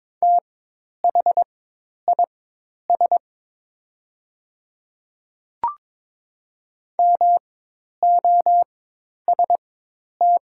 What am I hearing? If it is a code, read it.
Morse code, heard as a keyed tone, about 22 wpm, spaced as for 10 wpm, 700 Hz. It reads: THIS MOST
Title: 4x Character Spacing - Top 100 Words - 22wpm